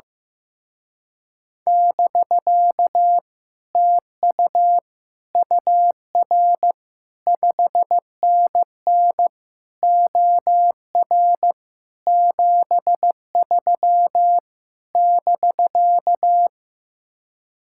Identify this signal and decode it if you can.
Morse code, heard as a keyed tone, about 15 wpm, 700 Hz. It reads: <BK> TU UR 5NN OR 73 <BK>
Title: Speed-Racing Common QSO Phrases Encore - 15wpm